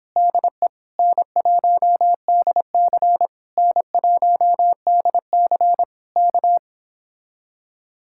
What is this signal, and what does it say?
Morse code, heard as a keyed tone, about 30 wpm, 700 Hz. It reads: DE N1DC N1DC K